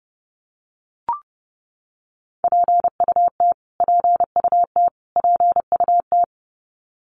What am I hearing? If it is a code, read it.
Morse code, heard as a keyed tone, about 30 wpm, 700 Hz. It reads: PUT PUT PUT
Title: Top 300 Words Repeated 30wpm